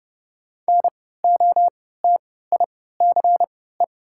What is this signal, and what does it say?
Morse code, heard as a keyed tone, about 30 wpm, spaced as for 17 wpm, 700 Hz. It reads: NOTICE
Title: 3x Character Spacing - Top 500 Words - 30wpm